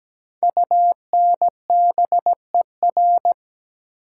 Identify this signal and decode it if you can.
Morse code, heard as a keyed tone, about 17 wpm, 700 Hz. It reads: UNBER
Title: Code Groups - 5 Letters - 17wpm